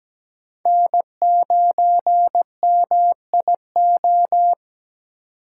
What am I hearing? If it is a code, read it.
Morse code, heard as a keyed tone, about 20 wpm, 700 Hz. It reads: N9MIO
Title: Speed Racing - USA 1x3 Call Signs 17wpm